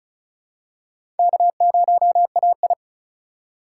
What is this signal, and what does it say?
Morse code, heard as a keyed tone, about 36 wpm, 700 Hz. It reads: K0AI
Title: USA Call Signs - 4 Characters - 2x Character Spacing - 35wpm